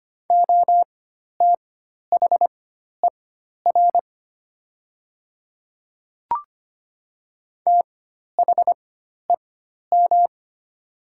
Morse code, heard as a keyed tone, about 25 wpm, spaced as for 12 wpm, 700 Hz. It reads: OTHER THEM
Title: 4x Character Spacing - Top 300 Words - 25wpm